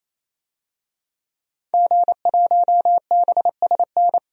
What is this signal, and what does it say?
Morse code, heard as a keyed tone, about 28 wpm, 700 Hz. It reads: G1BSN
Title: DXCC-100 Call Signs - 5 Characters - 28wpm